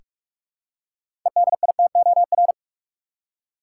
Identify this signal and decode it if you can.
Morse code, heard as a keyed tone, about 46 wpm, 700 Hz. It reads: EDITOR
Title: Six Letter Words - 45wpm